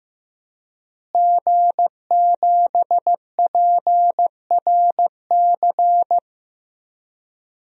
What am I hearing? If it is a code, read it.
Morse code, heard as a keyed tone, about 15 wpm, 700 Hz. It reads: G7PRC
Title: DXCC-100 Call Signs - 5 Characters - Repeated 2x - 15wpm